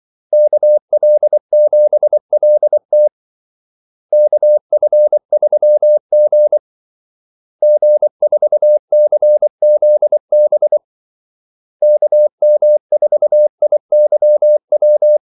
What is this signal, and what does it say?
Morse code, heard as a keyed tone, about 24 wpm, 600 Hz. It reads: KL7LT KF3G G4CZB KM4IYW
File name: Callsigns-24wpm